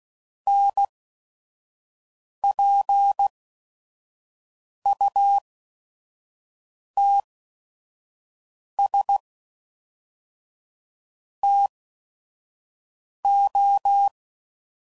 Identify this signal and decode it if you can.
Morse code, heard as a keyed tone, about 16 wpm, spaced as for 5 wpm, 800 Hz. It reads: NPUTS TO
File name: CW Code Groups @5WPM / Technical Subjects / AND Gate